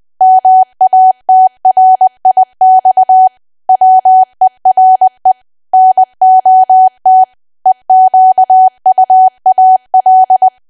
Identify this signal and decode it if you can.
Morse code, heard as a keyed tone, about 20 wpm, 750 Hz. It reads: MATRIX WERE NOT EQUAL
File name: cwa / W1AW-Level3 / 20WPM